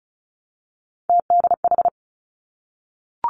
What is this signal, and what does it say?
Morse code, heard as a keyed tone, about 36 wpm, 700 Hz. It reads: TDH
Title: Three Challenging Letter-Number Combinations 35wpm